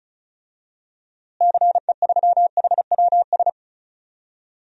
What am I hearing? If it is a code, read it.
Morse code, heard as a keyed tone, about 36 wpm, 700 Hz. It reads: CE3HWS
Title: DXCC-100 Call Signs - 6 Characters - Repeated 3x - 35wpm